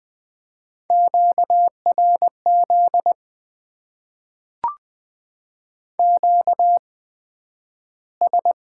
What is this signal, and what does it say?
Morse code, heard as a keyed tone, about 20 wpm, spaced as for 11 wpm, 700 Hz. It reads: QRZ QS
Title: Top 90 QSO Elements - 8x Character Spacing - 20wpm